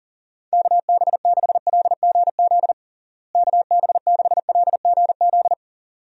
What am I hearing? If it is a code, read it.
Morse code, heard as a keyed tone, about 37 wpm, 700 Hz. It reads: KB6LGZ KB6LGZ
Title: USA Call Signs - 6 Characters - Repeated 2x - 40wpm